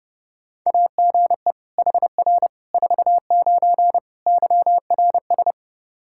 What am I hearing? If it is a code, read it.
Morse code, heard as a keyed tone, about 30 wpm, 700 Hz. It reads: AGE HR 49 YRS
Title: K1USN SST QSO Practice 30wpm